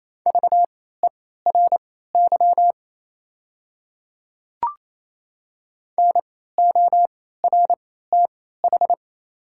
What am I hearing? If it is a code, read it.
Morse code, heard as a keyed tone, about 28 wpm, spaced as for 16 wpm, 700 Hz. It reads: VERY NORTH
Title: 3x Character Spacing - Top 300 Words - 28wpm